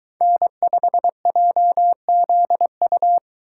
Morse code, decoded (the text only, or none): N5JZU